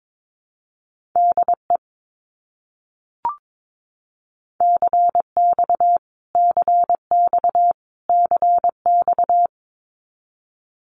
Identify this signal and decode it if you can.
Morse code, heard as a keyed tone, about 22 wpm, 700 Hz. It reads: DE CX CX CX
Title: Abbreviations Repeated 3x - 22wpm